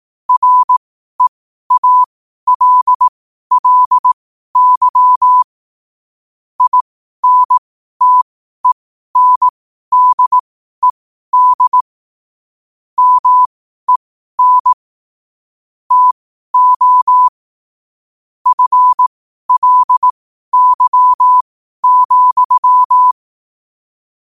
Morse code, heard as a keyed tone, about 18 wpm, spaced as for 12 wpm, 1000 Hz. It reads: REALLY INTENDED MEN TO FLY,
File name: qotdfiles / 12wpm